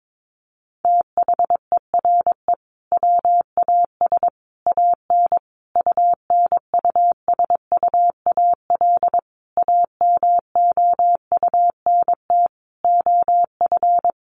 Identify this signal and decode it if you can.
Morse code, heard as a keyed tone, about 22 wpm, 700 Hz. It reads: THERE WAS AN UNUSUAL AMOUNT OF
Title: Sentences from Top 1400 Words 22wpm